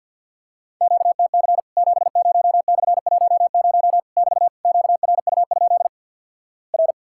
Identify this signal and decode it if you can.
Morse code, heard as a keyed tone, about 50 wpm, 700 Hz. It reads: QTC 70/10 <BT> QRU? R
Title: Worked All Europe DX Contest - QTC Exchange - 50wpm